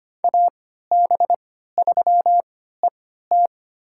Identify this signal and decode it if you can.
Morse code, heard as a keyed tone, about 25 wpm, spaced as for 14 wpm, 700 Hz. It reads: AB3ET
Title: USA Call Signs - 3x Character Spacing - 25wpm